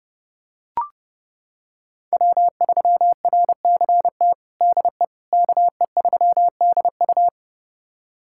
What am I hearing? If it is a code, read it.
Morse code, heard as a keyed tone, about 29 wpm, 700 Hz. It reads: W3RCT DE KE3DU
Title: USA Callsign DE Callsign - 5 Characters - 30wpm